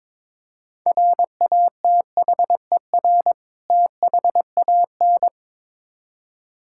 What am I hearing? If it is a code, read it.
Morse code, heard as a keyed tone, about 22 wpm, 700 Hz. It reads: RATHER THAN
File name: Build A Phrase - 2-Word Phrases - 22wpm